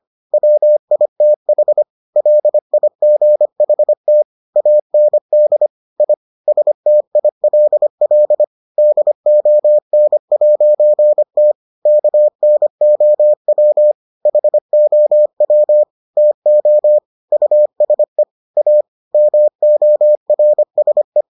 Morse code, heard as a keyed tone, about 25 wpm, 600 Hz. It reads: WITH LIGHT AND I STILL DON'T KNOW HOW TO USE A MORSE